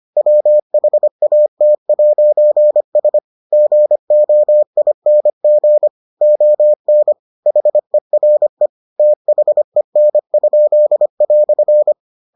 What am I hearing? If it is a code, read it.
Morse code, heard as a keyed tone, about 25 wpm, 600 Hz. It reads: WHAT'S GOING ON HERE THEN?"